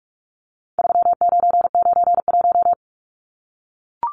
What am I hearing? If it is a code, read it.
Morse code, heard as a keyed tone, about 44 wpm, 700 Hz. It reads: 3991
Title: Numbers - 4-Digit 45wpm